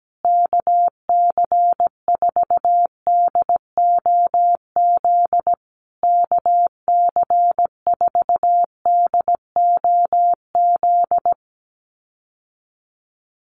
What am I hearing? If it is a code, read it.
Morse code, heard as a keyed tone, about 17 wpm, 700 Hz. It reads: KC4DOZ KC4DOZ
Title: USA Call Signs Repeated 3x 17wpm